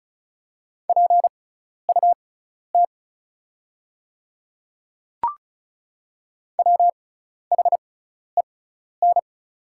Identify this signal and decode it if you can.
Morse code, heard as a keyed tone, about 36 wpm, spaced as for 12 wpm, 700 Hz. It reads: PUT WHEN
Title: 6x Character Spacing - Top 200 Words - 35wpm